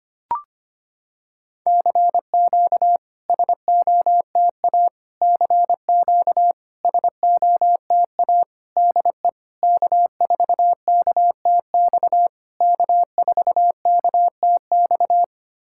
Morse code, heard as a keyed tone, about 25 wpm, 700 Hz. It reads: CQ SOTA CQ SOTA DE K4KTX K4KTX
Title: SOTA QSO Practice 25wpm